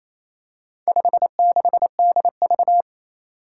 Morse code, heard as a keyed tone, about 27 wpm, 700 Hz. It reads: 56DV
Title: Four Challenging Letter-Number Combinations 28wpm